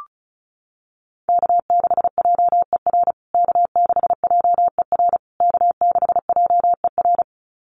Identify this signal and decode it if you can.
Morse code, heard as a keyed tone, about 34 wpm, 700 Hz. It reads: K6JER K6JER K6JER
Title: USA Call Signs Repeated 3x 35wpm